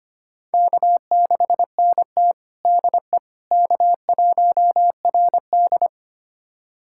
Morse code, heard as a keyed tone, about 25 wpm, 700 Hz. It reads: K6NT DE K1RD
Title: USA Callsign DE Callsign - 4 Characters - 25wpm